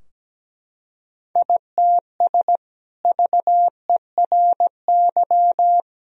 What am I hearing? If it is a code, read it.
Morse code, heard as a keyed tone, about 17 wpm, 700 Hz. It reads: ITS VERY
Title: Sentences from Top 1000 Words - Review Part 2 - 17wpm